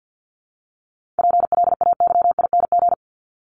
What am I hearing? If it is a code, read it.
Morse code, heard as a keyed tone, about 50 wpm, 700 Hz. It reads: PLAYING
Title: ING Suffix - Top 100 Words with Base Words 50wpm